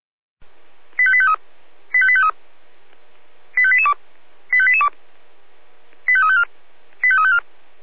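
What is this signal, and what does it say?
Five-tone selective call (ZVEI), heard as quick runs of five tones, tones of about 70 ms. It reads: 85853 85853 85882 85882 85354 85354
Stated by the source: ZVEI five-tone call 25427 received on SDR.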